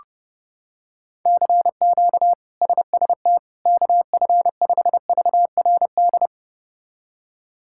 Morse code, heard as a keyed tone, about 32 wpm, 700 Hz. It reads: CQ SST KF5VRD
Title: K1USN SST QSO Practice - 30wpm